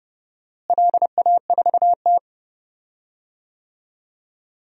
Morse code, heard as a keyed tone, about 29 wpm, 700 Hz. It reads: LA4T
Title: DXCC-100 Call Signs - 4 Characters - Repeated 3x - 30wpm